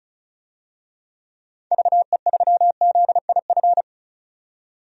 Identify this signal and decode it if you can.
Morse code, heard as a keyed tone, about 34 wpm, 700 Hz. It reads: VE3ZIF
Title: DXCC-100 Call Signs - 6 Characters - 35wpm